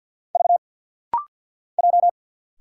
Morse code, heard as a keyed tone, about 50 wpm, 700 Hz. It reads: V J
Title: Single Letter-Number Alt - Rapid-Fire 50wpm